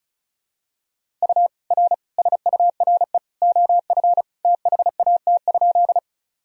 Morse code, heard as a keyed tone, about 36 wpm, 700 Hz. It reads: U R SURE OF THAT?
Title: Sentences from Top 300 Words and Abbreviations 35wpm